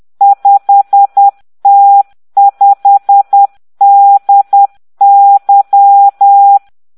Morse code, heard as a keyed tone, about 10 wpm, 800 Hz. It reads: 5T5DY